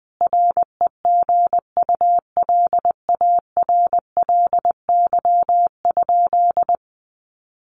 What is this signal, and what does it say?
Morse code, heard as a keyed tone, about 20 wpm, 700 Hz. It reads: REGULARLY?
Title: Sentences from Top 1400 Words 20wpm